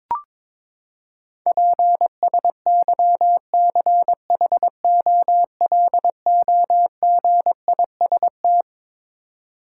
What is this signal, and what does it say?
Morse code, heard as a keyed tone, about 22 wpm, 700 Hz. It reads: PSYCHOLOGIST
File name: Twelve Letter Words - 22wpm